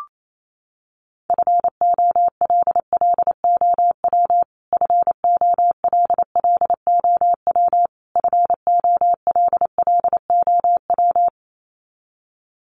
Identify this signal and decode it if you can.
Morse code, heard as a keyed tone, about 28 wpm, 700 Hz. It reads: FOLLOW FOLLOW FOLLOW
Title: Top 300 Words Repeated 28wpm